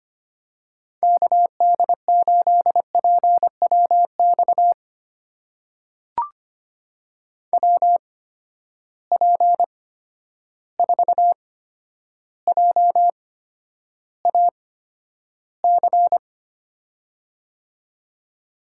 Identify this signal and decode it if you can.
Morse code, heard as a keyed tone, about 25 wpm, spaced as for 11 wpm, 700 Hz. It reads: KD8PWX WP4JAC
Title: USA Call Signs - 6 Characters - 8x Character Spacing - 25wpm